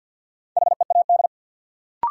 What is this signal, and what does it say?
Morse code, heard as a keyed tone, about 51 wpm, 700 Hz. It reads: HEAD
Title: Sentences from Top 300 Words - New Word Drill - 50wpm